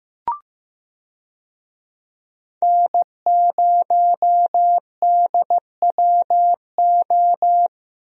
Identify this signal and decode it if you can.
Morse code, heard as a keyed tone, about 15 wpm, 700 Hz. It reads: N0DWO
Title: USA Call Signs - 5 Characters - Repeated 2x - 15wpm